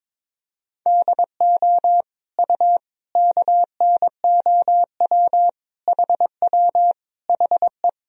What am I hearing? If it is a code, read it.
Morse code, heard as a keyed tone, about 22 wpm, 700 Hz. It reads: DO U KNOW HW HE
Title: Sentences from Top 300 Words and Abbreviations 22wpm